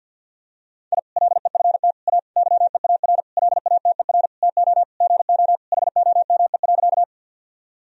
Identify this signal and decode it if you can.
Morse code, heard as a keyed tone, about 48 wpm, 700 Hz. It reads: I LEFT A YEAR LATER TO GO HOME.
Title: Sentences from Top 1000 Words - Review Part 4 - 50wpm